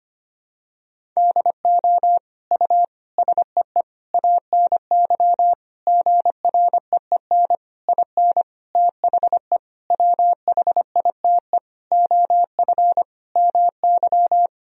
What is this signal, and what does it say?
Morse code, heard as a keyed tone, about 25 wpm, 700 Hz. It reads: DO U SEE ANY GREEN IN THE WHITE OF MY
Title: Sentences from Top 500 Words and Abbreviations 25wpm